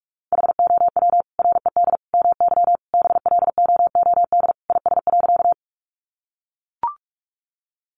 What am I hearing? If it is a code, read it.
Morse code, heard as a keyed tone, about 44 wpm, 700 Hz. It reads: HOW RED MY BLOOD IS.
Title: Sentences from Top 1100 Words 45wpm